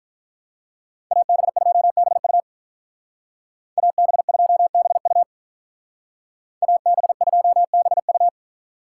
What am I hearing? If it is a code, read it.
Morse code, heard as a keyed tone, about 46 wpm, 700 Hz. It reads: AB2BU AB2BU AB2BU